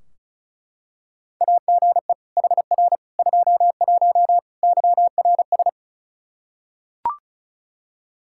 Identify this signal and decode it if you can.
Morse code, heard as a keyed tone, about 34 wpm, 700 Hz. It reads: AGE HR 21 YRS
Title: K1USN SST QSO Practice 35wpm